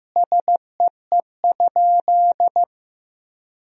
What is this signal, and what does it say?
Morse code, heard as a keyed tone, about 15 wpm, 700 Hz. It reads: SEE?
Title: Sentences from Top 100 Words and Abbreviations 15wpm